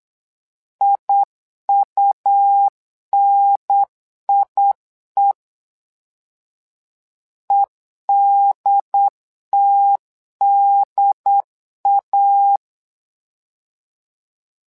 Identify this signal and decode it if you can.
Morse code, heard as a keyed tone, about 8 wpm, 800 Hz. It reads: IUNIE EDTDA
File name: LEZIONI / 2 UHDN / 2 UHDN 8wpm2232